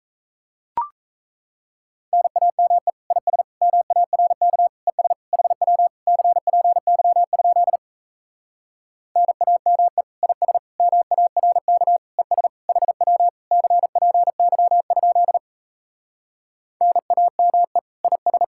Morse code, heard as a keyed tone, about 37 wpm, 700 Hz. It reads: NAME IS MARK ES HW CPY? NAME IS MARK ES HW CPY? NAME IS